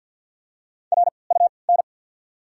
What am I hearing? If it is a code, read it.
Morse code, heard as a keyed tone, about 53 wpm, spaced as for 29 wpm, 700 Hz. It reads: RUN
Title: 3x Character Spacing - Top 500 Words - 50wpm